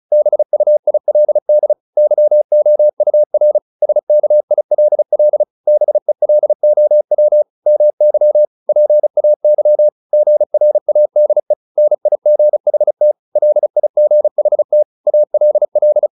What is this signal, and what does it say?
Morse code, heard as a keyed tone, about 34 wpm, 600 Hz. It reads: BUILD YOUR SKILL BELOW MY PAY GRADE NIGHT LIGHT ALL